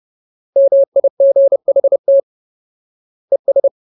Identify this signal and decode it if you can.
Morse code, heard as a keyed tone, about 30 wpm, 550 Hz. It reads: MIGHT ES
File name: Ultimate Word List / 30WPM Ultimate